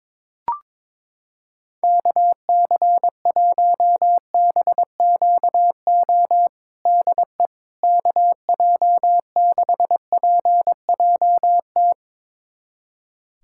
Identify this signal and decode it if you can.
Morse code, heard as a keyed tone, about 22 wpm, 700 Hz. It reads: KC1BQO DE KJ6PJT
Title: USA Callsign DE Callsign - 6 Characters - 22wpm